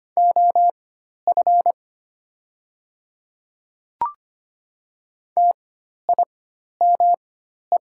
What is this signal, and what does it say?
Morse code, heard as a keyed tone, about 25 wpm, spaced as for 12 wpm, 700 Hz. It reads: OF TIME
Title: 4x Character Spacing - Top 100 Words - 25wpm